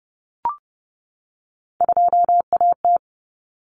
Morse code, heard as a keyed tone, about 30 wpm, 700 Hz. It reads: 2AT
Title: Common Cut Numbers - 3-Digit 30wpm